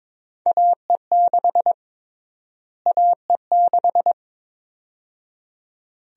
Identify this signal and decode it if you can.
Morse code, heard as a keyed tone, about 22 wpm, 700 Hz. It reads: AE6 AE6
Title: USA Call Sign Prefixes - Repeated 2x - 22wpm